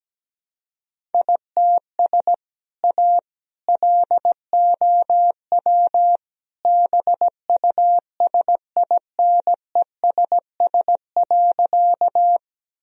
Morse code, heard as a keyed tone, about 17 wpm, 700 Hz. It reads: ITS A LOW BUSINESS.
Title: Sentences from Top 1000 Words - Review Part 1 - 17wpm